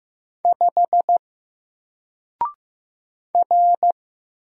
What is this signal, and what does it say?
Morse code, heard as a keyed tone, about 15 wpm, 700 Hz. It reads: 5 R
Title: Single Letter-Number Alt 15wpm